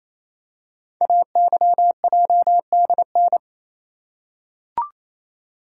Morse code, heard as a keyed tone, about 27 wpm, 700 Hz. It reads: AYJDN
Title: Code Groups - 5 Letters - 28wpm